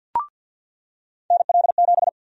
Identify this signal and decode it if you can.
Morse code, heard as a keyed tone, about 53 wpm, 700 Hz. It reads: NL7
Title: USA Call Sign Prefixes - 50wpm